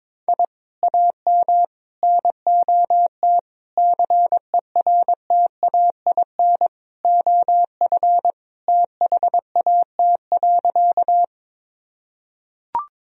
Morse code, heard as a keyed tone, about 22 wpm, 700 Hz. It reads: I AM NOT CERTAIN OF THAT.
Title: Sentences from Top 1000 Words - Review Part 2 - 22wpm